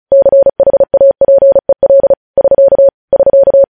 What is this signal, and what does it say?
Morse code, heard as a keyed tone, about 36 wpm, 550 Hz. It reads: CHAPEL <SK> <SK>
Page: news headlines in morse code at 35 wpm